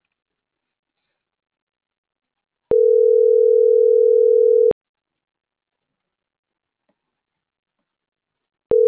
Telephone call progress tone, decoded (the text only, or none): ringback tone